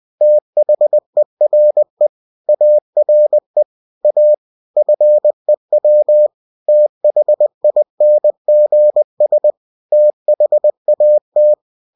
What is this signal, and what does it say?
Morse code, heard as a keyed tone, about 20 wpm, 600 Hz. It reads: THERE ARE A FEW THINGS THAT
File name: cw-books / winnie-the-pooh / winnie-20wpm-page-0113